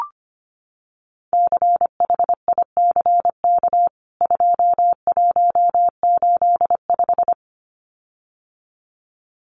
Morse code, heard as a keyed tone, about 25 wpm, 700 Hz. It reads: CHICK 2185
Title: CWT Exchange 25wpm